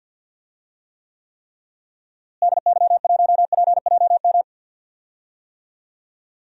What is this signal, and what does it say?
Morse code, heard as a keyed tone, about 46 wpm, 700 Hz. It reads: DY1PJM
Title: DXCC-100 Call Signs - 6 Characters - Repeated 3x - 50wpm